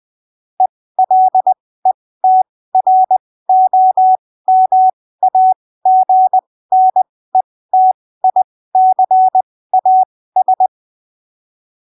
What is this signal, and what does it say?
Morse code, heard as a keyed tone, about 20 wpm, spaced as for 15 wpm, 750 Hz. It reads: ELETROMAGNETICAS